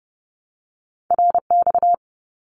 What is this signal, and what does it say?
Morse code, heard as a keyed tone, about 30 wpm, 700 Hz. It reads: RX
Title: Abbreviations Repeated 3x - 30wpm